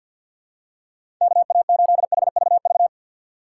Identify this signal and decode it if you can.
Morse code, heard as a keyed tone, about 47 wpm, 700 Hz. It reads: KA8HVV
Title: USA Call Signs - 6 Characters - 3x Character Spacing - 50wpm